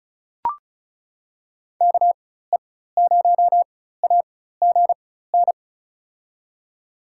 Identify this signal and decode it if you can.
Morse code, heard as a keyed tone, about 36 wpm, spaced as for 16 wpm, 700 Hz. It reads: KE0AGN
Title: USA Call Signs - 4x Character Spacing - 35wpm